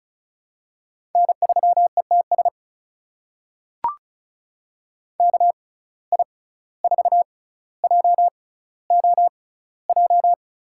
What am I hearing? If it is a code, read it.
Morse code, heard as a keyed tone, about 36 wpm, spaced as for 17 wpm, 700 Hz. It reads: N3ETS KI4JOJ
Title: USA Call Signs - 6x Character Spacing - 35wpm